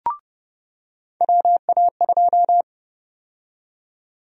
Morse code, heard as a keyed tone, about 30 wpm, 700 Hz. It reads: WA2
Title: USA Call Sign Prefixes - 30wpm